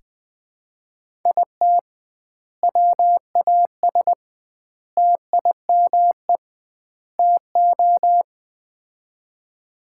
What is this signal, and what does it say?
Morse code, heard as a keyed tone, about 20 wpm, 700 Hz. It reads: IT WAS TIME TO
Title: 2x Word-Spacing - Sets of 4 Words - 20wpm